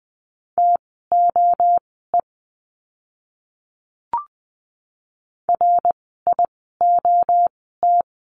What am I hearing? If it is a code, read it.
Morse code, heard as a keyed tone, about 20 wpm, spaced as for 14 wpm, 700 Hz. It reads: TOE RIOT
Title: 061 Lesson 11f - Letter h - Words 20wpm